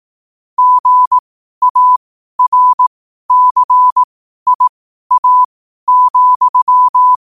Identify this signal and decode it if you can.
Morse code, heard as a keyed tone, about 18 wpm, spaced as for 12 wpm, 1000 Hz. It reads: GARCIA,
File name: qotdfiles / 12wpm